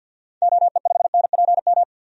Morse code, heard as a keyed tone, about 52 wpm, 700 Hz. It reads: OE5NPM
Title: DXCC-100 Call Signs - 6 Characters - 50wpm